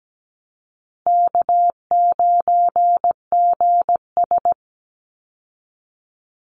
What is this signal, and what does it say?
Morse code, heard as a keyed tone, about 17 wpm, 700 Hz. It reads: K9GS
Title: Easy Call Signs Repeated 3x - 17wpm